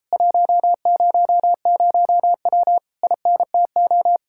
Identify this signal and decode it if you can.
Morse code, heard as a keyed tone, about 37 wpm, 700 Hz. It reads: 100W INTO